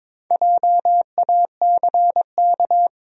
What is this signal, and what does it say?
Morse code, heard as a keyed tone, about 22 wpm, 700 Hz. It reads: JACK
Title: USA Common Names - 22wpm